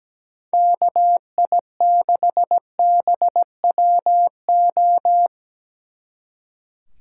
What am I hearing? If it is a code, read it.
Morse code, heard as a keyed tone, about 18 wpm, 700 Hz. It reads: KI6BWO
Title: Speed Racing - USA 2x3 Call Signs 17wpm